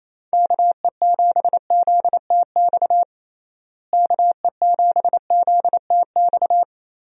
Morse code, heard as a keyed tone, about 28 wpm, 700 Hz. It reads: KE7ZTX KE7ZTX
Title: USA Call Signs - 6 Characters - Repeated 3x - 28wpm